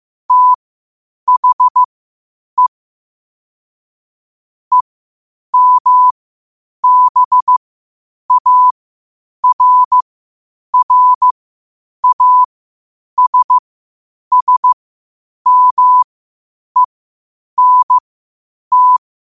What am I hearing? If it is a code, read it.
Morse code, heard as a keyed tone, about 15 wpm, spaced as for 8 wpm, 1000 Hz. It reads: THE EMBARRASSMENT
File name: qotdfiles / 8wpm